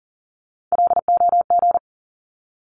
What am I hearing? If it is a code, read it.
Morse code, heard as a keyed tone, about 40 wpm, 700 Hz. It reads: LOG
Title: Top 1251-1500 Words 40wpm